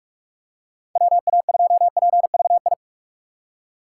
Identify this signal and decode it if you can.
Morse code, heard as a keyed tone, about 46 wpm, 700 Hz. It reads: WA2PVI
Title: USA Call Signs - 6 Characters - 4x Character Spacing - 45wpm